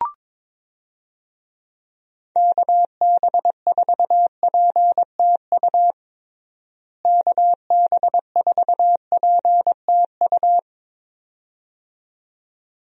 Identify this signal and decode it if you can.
Morse code, heard as a keyed tone, about 22 wpm, 700 Hz. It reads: KB4PTU KB4PTU
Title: USA Call Signs - 6 Characters - Repeated 2x - 22wpm